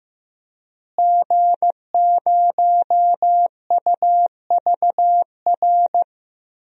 Morse code, heard as a keyed tone, about 16 wpm, 700 Hz. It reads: G0UVR
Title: DXCC-100 Call Signs - 5 Characters - Repeated 3x - 15wpm